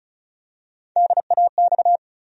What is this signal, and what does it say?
Morse code, heard as a keyed tone, about 36 wpm, 700 Hz. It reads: DAX
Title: USA Call Sign Postfixes - 3 Letters 35wpm